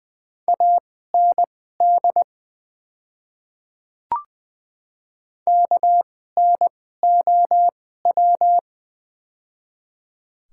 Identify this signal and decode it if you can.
Morse code, heard as a keyed tone, about 20 wpm, spaced as for 15 wpm, 700 Hz. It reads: AND KNOW
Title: 2x Character Spacing - Top 100 Words - 20wpm